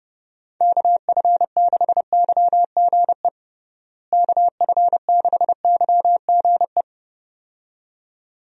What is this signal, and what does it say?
Morse code, heard as a keyed tone, about 30 wpm, 700 Hz. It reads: KF6YGE KF6YGE
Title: USA Call Signs - 6 Characters - Repeated 2x - 30wpm